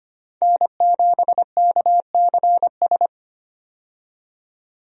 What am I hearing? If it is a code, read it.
Morse code, heard as a keyed tone, about 25 wpm, 700 Hz. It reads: N7KCS
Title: USA Call Signs - 5 Characters - 25wpm